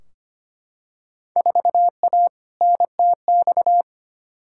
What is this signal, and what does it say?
Morse code, heard as a keyed tone, about 25 wpm, 700 Hz. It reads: 4A NTX
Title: Field Day Exchanges - 25wpm